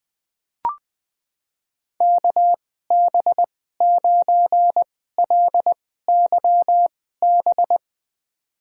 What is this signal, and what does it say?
Morse code, heard as a keyed tone, about 21 wpm, spaced as for 15 wpm, 700 Hz. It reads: KB9LYB
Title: USA Call Signs - 6 Characters - 2x Character Spacing - 20wpm